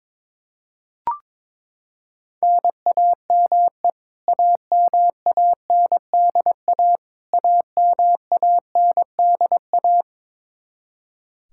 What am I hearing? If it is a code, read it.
Morse code, heard as a keyed tone, about 22 wpm, 700 Hz. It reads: NAME AMANDA AMANDA
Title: K1USN SST QSO Practice 22wpm